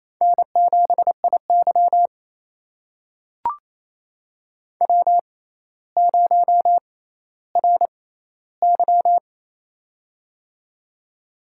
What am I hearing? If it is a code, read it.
Morse code, heard as a keyed tone, about 28 wpm, spaced as for 14 wpm, 700 Hz. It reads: N7IY W0RY
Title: USA Call Signs - 4 Characters - 6x Character Spacing - 28wpm